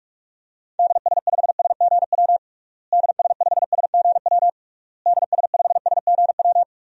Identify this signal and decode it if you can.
Morse code, heard as a keyed tone, about 41 wpm, 700 Hz. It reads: DS5SGW DS5SGW DS5SGW
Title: DXCC-100 Call Signs - 6 Characters - Repeated 3x - 45wpm